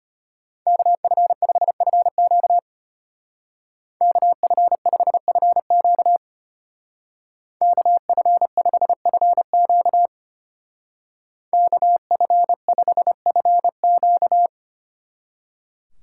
Morse code, heard as a keyed tone, about 30 wpm, 700 Hz. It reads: KF5FQ KF5FQ KF5FQ KF5FQ
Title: Speed Racing - USA 2x2 Call Signs 25wpm